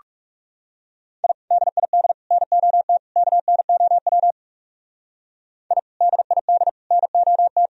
Morse code, heard as a keyed tone, about 34 wpm, 700 Hz. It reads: I DID NOT KNOW I DID NOT
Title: Speed-Racing - Sets of 4 Words 30wpm